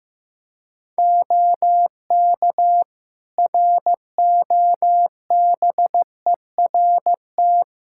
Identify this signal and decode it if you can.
Morse code, heard as a keyed tone, about 15 wpm, 700 Hz. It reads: OK ROBERT
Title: K1USN SST QSO Practice 15wpm